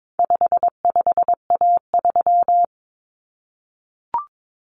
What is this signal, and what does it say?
Morse code, heard as a keyed tone, about 22 wpm, 700 Hz. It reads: 55A3
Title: Common Cut Numbers - 4-Digit 22wpm